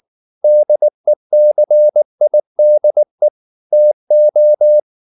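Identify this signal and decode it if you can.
Morse code, heard as a keyed tone, about 19 wpm, 600 Hz. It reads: DECIDE TO
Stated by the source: Morse code at 19 wpm from CW med Gnister, reading Ken Follett's novel Never in English.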